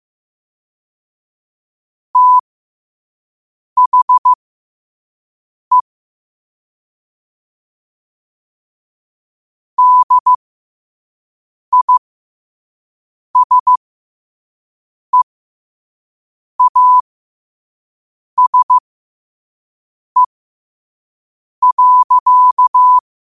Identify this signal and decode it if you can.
Morse code, heard as a keyed tone, about 15 wpm, spaced as for 5 wpm, 1000 Hz. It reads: THE DISEASE.
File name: qotdfiles / 5wpm